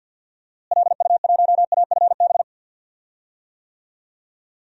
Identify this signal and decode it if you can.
Morse code, heard as a keyed tone, about 50 wpm, 700 Hz. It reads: LU1AFB
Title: DXCC-100 Call Signs - 6 Characters - Repeated 3x - 50wpm